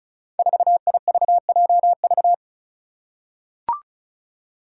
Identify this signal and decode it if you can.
Morse code, heard as a keyed tone, about 35 wpm, 700 Hz. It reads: 4IVJV
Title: Five Challenging Letter-Number Combinations 35wpm